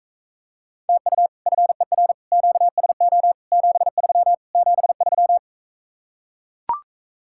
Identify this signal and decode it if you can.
Morse code, heard as a keyed tone, about 44 wpm, 700 Hz. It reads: TU FER QSO 73 73